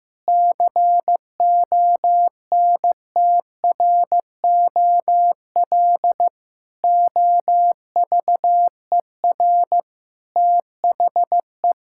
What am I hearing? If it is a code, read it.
Morse code, heard as a keyed tone, about 16 wpm, 700 Hz. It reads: CONTROL OVER THE
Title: Sentences from Top 1000 Words - Review Part 5 - 15wpm